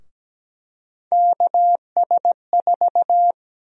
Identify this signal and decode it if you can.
Morse code, heard as a keyed tone, about 17 wpm, 700 Hz. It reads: KS4
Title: USA Call Sign Prefixes - 17wpm